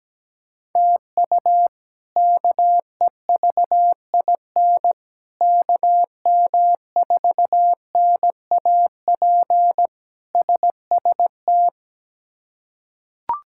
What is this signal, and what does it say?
Morse code, heard as a keyed tone, about 17 wpm, 700 Hz. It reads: TU KEVIN KM4NAP SST
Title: K1USN SST QSO Practice - 17wpm